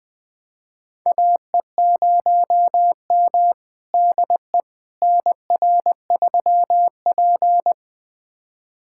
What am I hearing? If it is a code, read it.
Morse code, heard as a keyed tone, about 20 wpm, 700 Hz. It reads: AE0M DE NR3P
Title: USA Callsign DE Callsign - 4 Characters - 20wpm